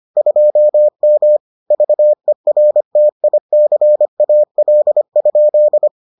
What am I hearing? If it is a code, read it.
Morse code, heard as a keyed tone, about 25 wpm, 600 Hz. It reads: 2M VERTICAL?